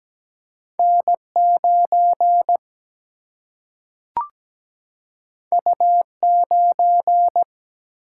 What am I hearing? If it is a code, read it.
Morse code, heard as a keyed tone, about 17 wpm, 700 Hz. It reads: N9 U9
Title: Single-Letter Single-Number 17wpm